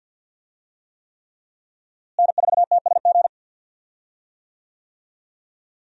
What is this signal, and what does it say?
Morse code, heard as a keyed tone, about 52 wpm, 700 Hz. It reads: N4TSG
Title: USA Call Signs - 5 Characters - Repeated 3x - 50wpm